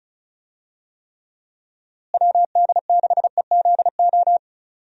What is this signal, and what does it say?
Morse code, heard as a keyed tone, about 34 wpm, 700 Hz. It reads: WD6EZO
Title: USA Call Signs - 6 Characters - Repeated 3x - 35wpm